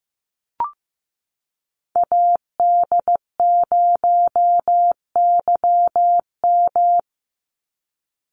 Morse code, heard as a keyed tone, about 15 wpm, 700 Hz. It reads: AD0YM
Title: NA SOTA Chaser Callsigns 15wpm